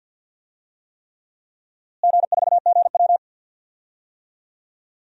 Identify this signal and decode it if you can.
Morse code, heard as a keyed tone, about 48 wpm, 700 Hz. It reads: G4GW
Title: DXCC-100 Call Signs - 4 Characters - Repeated 3x - 50wpm